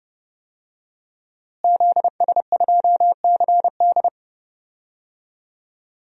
Morse code, heard as a keyed tone, about 30 wpm, 700 Hz. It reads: ZS2CD
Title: DXCC-100 Call Signs - 5 Characters - Repeated 2x - 30wpm